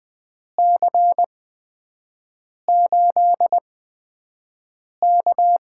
Morse code, heard as a keyed tone, about 20 wpm, spaced as for 5 wpm, 700 Hz. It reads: C8K
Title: USA Call Signs - 8x Character Spacing - 20wpm